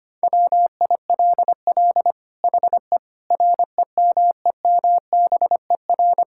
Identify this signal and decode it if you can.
Morse code, heard as a keyed tone, about 25 wpm, 700 Hz. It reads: WILL HE REMEMBER
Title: Sentences from Top 1000 Words - Review Part 6 - 25wpm